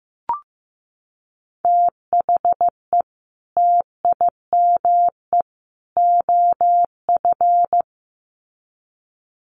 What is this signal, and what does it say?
Morse code, heard as a keyed tone, about 15 wpm, 700 Hz. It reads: THE TIME OF
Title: Sets of 3 Words 15wpm